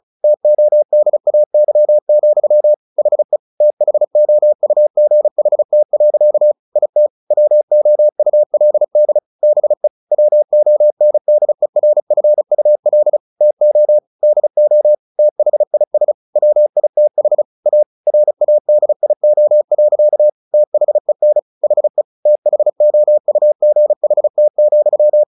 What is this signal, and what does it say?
Morse code, heard as a keyed tone, about 34 wpm, 600 Hz. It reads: TODAY, HE THOUGHT. IT WOULD BE WONDERFUL TO DO THIS WITH A RADIO. THEN HE THOUGHT,
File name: the old man and the sea / the old man and the sea 1 35wpm